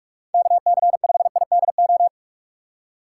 Morse code, heard as a keyed tone, about 46 wpm, 700 Hz. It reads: KC5IDO
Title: USA Call Signs - 2x Character Spacing - 45wpm